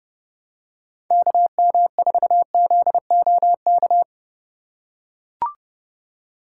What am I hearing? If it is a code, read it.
Morse code, heard as a keyed tone, about 30 wpm, 700 Hz. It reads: KM4ZOK
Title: USA Call Signs - 8x Character Spacing - 30wpm